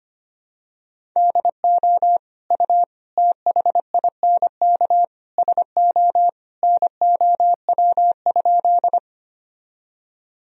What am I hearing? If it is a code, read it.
Morse code, heard as a keyed tone, about 25 wpm, 700 Hz. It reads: DO U THINK SO NOW?
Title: Sentences from Top 200 Words and Abbreviations 25wpm